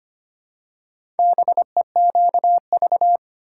Morse code, heard as a keyed tone, about 25 wpm, 700 Hz. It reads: BEQV
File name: practice / 25wpm / Four Letter-Number Combinations - 25wpm